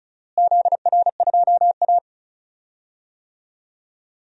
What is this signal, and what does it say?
Morse code, heard as a keyed tone, about 36 wpm, 700 Hz. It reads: ZR2A
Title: DXCC-100 Call Signs - 4 Characters - Repeated 2x - 35wpm